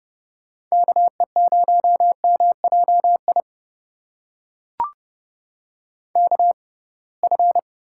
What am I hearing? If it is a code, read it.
Morse code, heard as a keyed tone, about 31 wpm, spaced as for 23 wpm, 700 Hz. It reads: KE0MJI KF
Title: USA Call Signs - 6 Characters - 6x Character Spacing - 30wpm